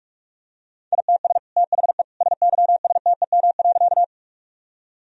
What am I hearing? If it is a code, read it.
Morse code, heard as a keyed tone, about 41 wpm, 700 Hz. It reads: ITS THE SYSTEM.
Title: Sentences from Top 1000 Words - Review Part 3 - 45wpm